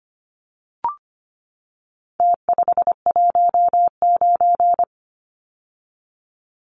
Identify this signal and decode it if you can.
Morse code, heard as a keyed tone, about 25 wpm, 700 Hz. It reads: T519
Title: Numbers - Contest Style - 4-Digit 25wpm